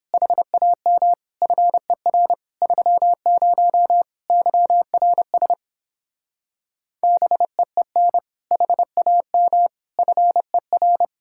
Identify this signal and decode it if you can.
Morse code, heard as a keyed tone, about 29 wpm, 700 Hz. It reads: HAM FER 30 YRS BEEN HAM FER